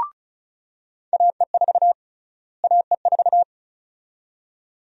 Morse code, heard as a keyed tone, about 36 wpm, 700 Hz. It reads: AE4 AE4
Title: USA Call Sign Prefixes - Repeated 2x - 35wpm